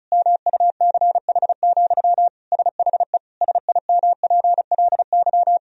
Morse code, heard as a keyed tone, about 36 wpm, 700 Hz. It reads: MUCH, SHE SIMPLY
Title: Sentences from Top 1000 Words - Review Part 4 - 35wpm